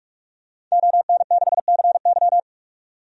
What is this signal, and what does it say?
Morse code, heard as a keyed tone, about 45 wpm, 700 Hz. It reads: ON6CY
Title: DXCC-100 Call Signs - 5 Characters - 45wpm